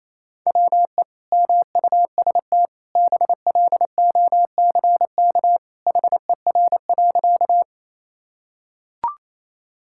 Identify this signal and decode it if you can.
Morse code, heard as a keyed tone, about 28 wpm, 700 Hz. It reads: WE MUST BLOCK HER.
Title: Sentences from Top 1600 Words 28wpm